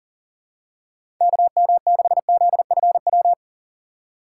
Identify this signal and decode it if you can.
Morse code, heard as a keyed tone, about 39 wpm, 700 Hz. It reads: KM6ZFW
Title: USA Call Signs - 6 Characters - 2x Character Spacing - 40wpm